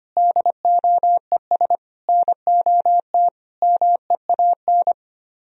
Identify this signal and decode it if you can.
Morse code, heard as a keyed tone, about 26 wpm, 700 Hz. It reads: DOES NOT MEAN
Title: Speed-Racing - Sets of 5 Words 25wpm